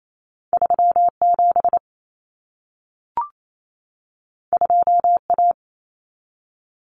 Morse code, heard as a keyed tone, about 28 wpm, 700 Hz. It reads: 37 2A
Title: Common Cut Numbers - 2-Digit 28wpm